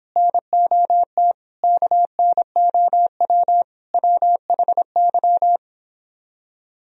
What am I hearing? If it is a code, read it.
Morse code, heard as a keyed tone, about 25 wpm, 700 Hz. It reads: NOT KNOW WHY